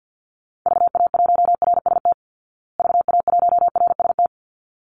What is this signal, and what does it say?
Morse code, heard as a keyed tone, about 50 wpm, 700 Hz. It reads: VA1RST VA1RST
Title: Call Signs Repeated 3x - 50wpm